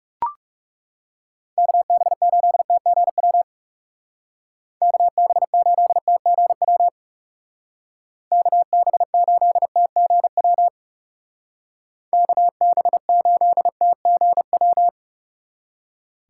Morse code, heard as a keyed tone, about 37 wpm, 700 Hz. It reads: KB8TGW KB8TGW KB8TGW KB8TGW
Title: Speed Racing - USA 2x3 Call Signs 30wpm